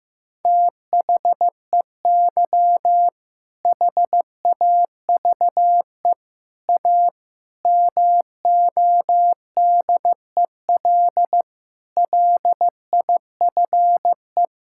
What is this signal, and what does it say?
Morse code, heard as a keyed tone, about 15 wpm, 700 Hz. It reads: THEY HAVE A MODEL LIFE
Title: Sentences from Top 1000 Words - Review Part 5 - 15wpm